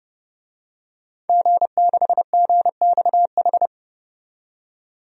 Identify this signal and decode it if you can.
Morse code, heard as a keyed tone, about 30 wpm, 700 Hz. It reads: G6GXH
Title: DXCC-100 Call Signs - 5 Characters - 30wpm